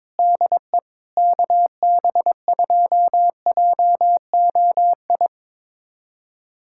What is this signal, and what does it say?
Morse code, heard as a keyed tone, about 22 wpm, 700 Hz. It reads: DE KB2JOI
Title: USA Callsign DE Callsign - 6 Characters - 22wpm